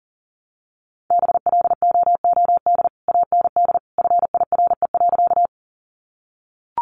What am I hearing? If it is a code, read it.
Morse code, heard as a keyed tone, about 40 wpm, 700 Hz. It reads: BLOOD AND FIRE.
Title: Sentences from Top 1100 Words 40wpm